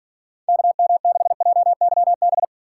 Morse code, heard as a keyed tone, about 50 wpm, 700 Hz. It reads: KM6JYB
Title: Speed Racing - USA 2x3 Call Signs 40wpm